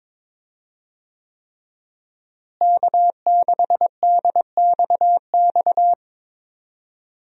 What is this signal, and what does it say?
Morse code, heard as a keyed tone, about 22 wpm, 700 Hz. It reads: K6DXX
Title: USA Call Signs - 5 Characters - Repeated 3x - 22wpm